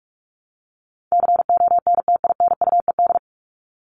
Morse code, heard as a keyed tone, about 44 wpm, 700 Hz. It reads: CONTINUED
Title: ED Suffix - Top 100 Words with Base Words 45wpm